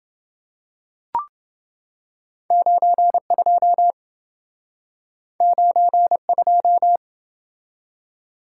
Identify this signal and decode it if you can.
Morse code, heard as a keyed tone, about 28 wpm, 700 Hz. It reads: 92 92